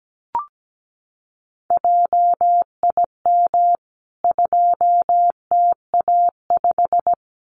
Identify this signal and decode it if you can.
Morse code, heard as a keyed tone, about 17 wpm, 700 Hz. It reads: JIM 2TA5
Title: CWT Exchange 17wpm